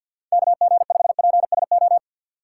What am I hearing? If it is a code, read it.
Morse code, heard as a keyed tone, about 52 wpm, 700 Hz. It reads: KG5PSO
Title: USA Call Signs - 6 Characters - 3x Character Spacing - 50wpm